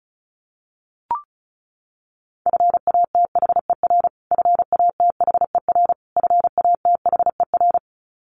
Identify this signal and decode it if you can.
Morse code, heard as a keyed tone, about 34 wpm, 700 Hz. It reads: FATHER FATHER FATHER
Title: Top 500 Words Repeated 35wpm